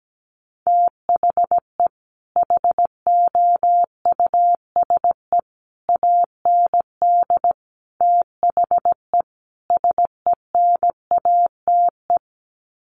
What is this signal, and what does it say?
Morse code, heard as a keyed tone, about 17 wpm, 700 Hz. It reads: THE HOUSE AND THE SENATE
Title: Sets of 5 Words Encore2 17wpm